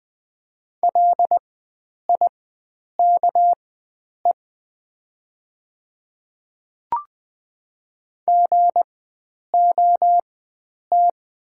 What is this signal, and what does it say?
Morse code, heard as a keyed tone, about 20 wpm, spaced as for 9 wpm, 700 Hz. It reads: LIKE GOT
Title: 4x Character Spacing - Top 300 Words - 20wpm